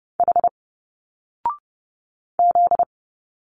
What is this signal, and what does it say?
Morse code, heard as a keyed tone, about 30 wpm, 700 Hz. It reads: H Z
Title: Single Letters 30wpm